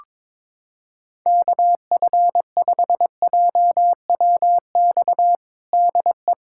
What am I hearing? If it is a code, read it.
Morse code, heard as a keyed tone, about 22 wpm, 700 Hz. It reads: KF5JWX DE
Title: USA Callsign DE Callsign 22wpm